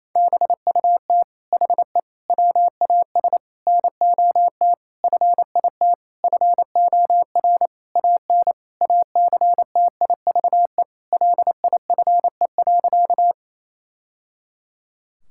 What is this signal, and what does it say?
Morse code, heard as a keyed tone, about 28 wpm, 700 Hz. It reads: BUT HE WAS NOT FIT FOR AN ACTIVE LIFE.
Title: Sentences from Top 1000 Words - Review Part 6 - 28wpm